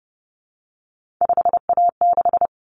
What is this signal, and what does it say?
Morse code, heard as a keyed tone, about 30 wpm, 700 Hz. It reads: 5A6
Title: Common Cut Numbers - 3-Digit 30wpm